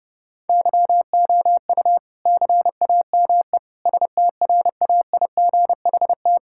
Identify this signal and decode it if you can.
Morse code, heard as a keyed tone, about 30 wpm, 700 Hz. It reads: YOU CAME STRAIGHT
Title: Sentences from Top 1000 Words - Review Part 6 - 30wpm